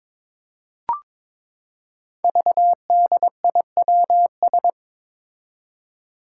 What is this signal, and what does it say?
Morse code, heard as a keyed tone, about 22 wpm, 700 Hz. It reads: VDIWS
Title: Five Challenging Letter-Number Combinations 22wpm